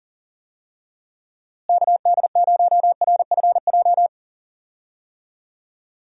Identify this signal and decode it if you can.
Morse code, heard as a keyed tone, about 39 wpm, 700 Hz. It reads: KD0RFJ
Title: USA Call Signs - 6 Characters - Repeated 3x - 40wpm